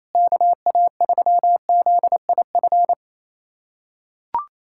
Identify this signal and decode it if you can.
Morse code, heard as a keyed tone, about 28 wpm, 700 Hz. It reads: KA3ZIF
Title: USA Callsign DE Callsign 28wpm